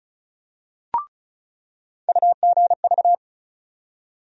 Morse code, heard as a keyed tone, about 36 wpm, 700 Hz. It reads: UGV
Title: USA Call Sign Postfixes - 3 Letters 35wpm